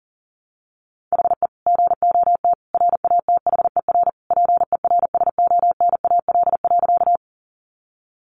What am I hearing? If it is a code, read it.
Morse code, heard as a keyed tone, about 37 wpm, 700 Hz. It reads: HE GOT RATHER PERSONAL.
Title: Sentences from Top 700 Words 40wpm